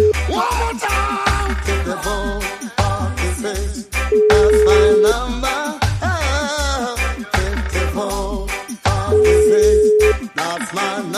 Ringback tone (Europe, 425 Hz), with noise.